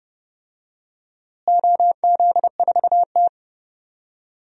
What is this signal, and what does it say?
Morse code, heard as a keyed tone, about 30 wpm, 700 Hz. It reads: OZ4T
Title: DXCC-100 Call Signs - 4 Characters - Repeated 2x - 30wpm